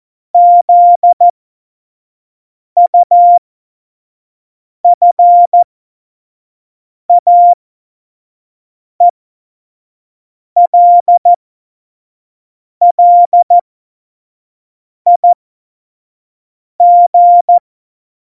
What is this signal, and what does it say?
Morse code, heard as a keyed tone, about 14 wpm, spaced as for 5 wpm, 700 Hz. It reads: ZUFAELLIG
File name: Morsekurs / Witze / Witz 29. Oktober 2019, 25 F70